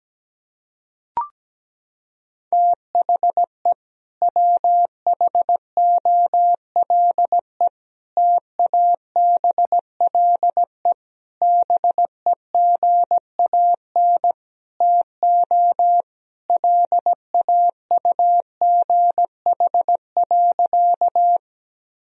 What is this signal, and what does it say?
Morse code, heard as a keyed tone, about 17 wpm, 700 Hz. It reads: THE WHOLE TABLE BEGAN TO LAUGH.
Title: Sentences from Top 1000 Words - Review Part 2 - 17wpm